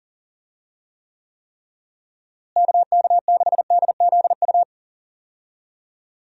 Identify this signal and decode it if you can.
Morse code, heard as a keyed tone, about 40 wpm, 700 Hz. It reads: KK6DZU
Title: USA Call Signs - 6 Characters - Repeated 3x - 40wpm